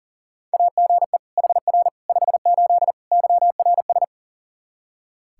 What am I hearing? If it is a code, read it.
Morse code, heard as a keyed tone, about 40 wpm, 700 Hz. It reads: AGE HR 58 YRS
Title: Practice QSO - Exchange 3 - More Info - 40wpm